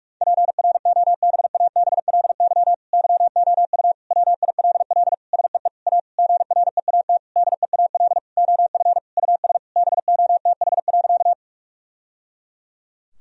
Morse code, heard as a keyed tone, about 47 wpm, 700 Hz. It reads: PROBABLY YOU WILL SEE A GREAT DEAL OF US BOTH.
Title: Sentences from Top 1000 Words - Review Part 4 - 45wpm